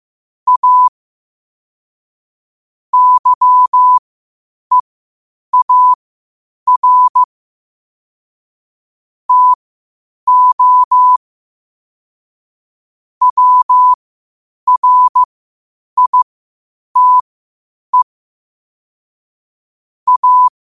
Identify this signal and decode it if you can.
Morse code, heard as a keyed tone, about 15 wpm, spaced as for 8 wpm, 1000 Hz. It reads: A YEAR TO WRITE A
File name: qotdfiles / 8wpm